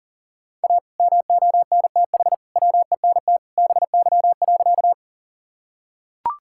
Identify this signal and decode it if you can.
Morse code, heard as a keyed tone, about 40 wpm, 700 Hz. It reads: A MONTH WENT BY.
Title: Sentences from Top 1000 Words - Review Part 5 - 40wpm